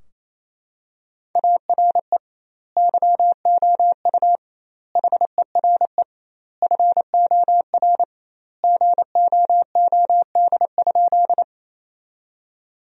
Morse code, heard as a keyed tone, about 28 wpm, 700 Hz. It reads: ARE YOU HERE FOR GOOD?
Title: 2x Word-Spacing - Sentences from Top 200 Words - 28wpm